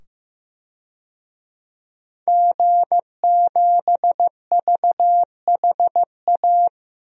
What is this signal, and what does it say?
Morse code, heard as a keyed tone, about 15 wpm, 700 Hz. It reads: G7VHA
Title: DXCC-100 Call Signs - 5 Characters - 15wpm